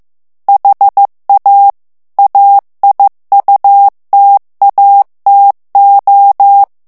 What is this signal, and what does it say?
Morse code, heard as a keyed tone, about 15 wpm, 800 Hz. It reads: HA AIUTATO
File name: ITALIEN texte clair 1 vitesse 14 mots minute